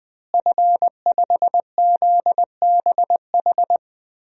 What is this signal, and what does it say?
Morse code, heard as a keyed tone, about 21 wpm, 700 Hz. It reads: F5ZBH
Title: DXCC-100 Call Signs - 5 Characters - Repeated 2x - 20wpm